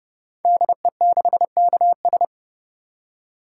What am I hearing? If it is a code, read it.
Morse code, heard as a keyed tone, about 28 wpm, 700 Hz. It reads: DE6KS
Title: Code Groups - 5 Letter:Numbers - 30wpm